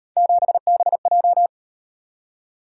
Morse code, heard as a keyed tone, about 35 wpm, 700 Hz. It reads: 7BJ